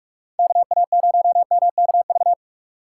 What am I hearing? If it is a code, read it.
Morse code, heard as a keyed tone, about 46 wpm, 700 Hz. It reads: KA0MKV
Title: USA Call Signs - 6 Characters - 4x Character Spacing - 45wpm